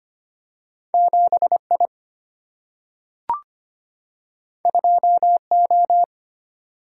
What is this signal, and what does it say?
Morse code, heard as a keyed tone, about 25 wpm, 700 Hz. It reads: 7 I 2 O
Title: Single-Letter Single-Number 25wpm